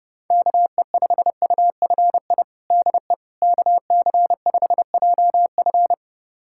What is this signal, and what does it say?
Morse code, heard as a keyed tone, about 30 wpm, 700 Hz. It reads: KE5UFI DE KC5JF
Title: USA Callsign DE Callsign 30wpm